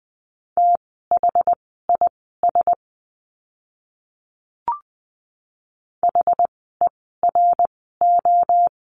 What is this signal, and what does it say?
Morse code, heard as a keyed tone, about 20 wpm, spaced as for 14 wpm, 700 Hz. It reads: THIS HERO
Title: 068 Lesson 12f - Letter d - Words 20wpm